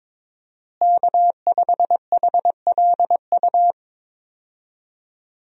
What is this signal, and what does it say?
Morse code, heard as a keyed tone, about 22 wpm, 700 Hz. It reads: K5HLU
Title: USA Call Signs - 5 Characters - Repeated 2x - 22wpm